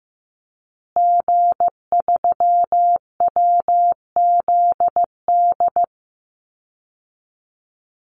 Morse code, heard as a keyed tone, about 15 wpm, 700 Hz. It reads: G3WZD
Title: Call Signs Repeated 2x - 15wpm